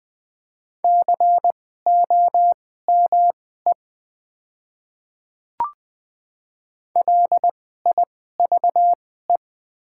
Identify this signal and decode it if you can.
Morse code, heard as a keyed tone, about 20 wpm, spaced as for 14 wpm, 700 Hz. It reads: COME LIVE
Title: 2x Character Spacing - Top 200 Words - 20wpm